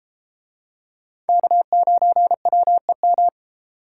Morse code, heard as a keyed tone, about 31 wpm, 700 Hz. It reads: K9WEM